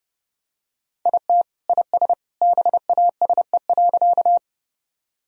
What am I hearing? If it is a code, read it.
Morse code, heard as a keyed tone, about 32 wpm, 700 Hz. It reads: IT IS BASE.
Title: Sentences from Top 1000 Words - Review Part 1 - 30wpm